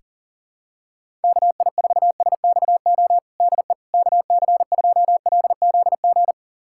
Morse code, heard as a keyed tone, about 37 wpm, 700 Hz. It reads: KI4SXO DE KC2LZG
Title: USA Callsign DE Callsign 40wpm